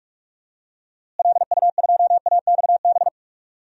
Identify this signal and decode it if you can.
Morse code, heard as a keyed tone, about 41 wpm, 700 Hz. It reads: LU2AXB
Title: DXCC-100 Call Signs - 6 Characters - Repeated 3x - 45wpm